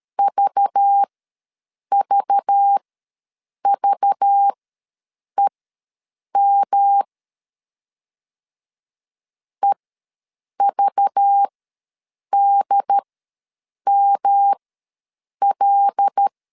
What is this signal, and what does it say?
Morse code, heard as a keyed tone, about 13 wpm, spaced as for 7 wpm, 800 Hz. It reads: VVVEM EVDML
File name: LEZIONI / 5 LBVC / 5 LBV ws 12wpm 3015